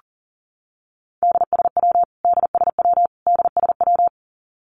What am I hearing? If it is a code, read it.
Morse code, heard as a keyed tone, about 40 wpm, 700 Hz. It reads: DSW DSW DSW
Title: Abbreviations Repeated 3x - 40wpm